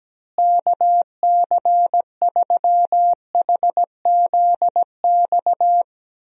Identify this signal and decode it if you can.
Morse code, heard as a keyed tone, about 17 wpm, 700 Hz. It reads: KC3HZX
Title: USA Call Signs - 6 Characters - Repeated 3x - 17wpm